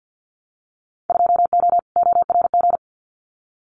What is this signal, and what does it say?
Morse code, heard as a keyed tone, about 50 wpm, 700 Hz. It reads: 2O ORG
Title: Winter Field Day Exchanges 50wpm